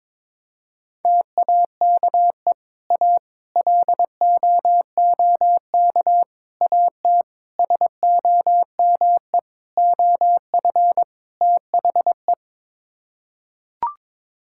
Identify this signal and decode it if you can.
Morse code, heard as a keyed tone, about 22 wpm, 700 Hz. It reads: TAKE A LOOK AT SOME OF THE
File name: practice / 22wpm / Sets of 7 Words - 22wpm